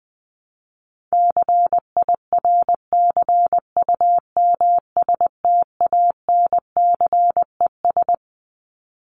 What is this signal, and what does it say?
Morse code, heard as a keyed tone, about 20 wpm, 700 Hz. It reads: CIRCUMSTANCES
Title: ES:S Suffix - Top 100 Words with Base Words 20wpm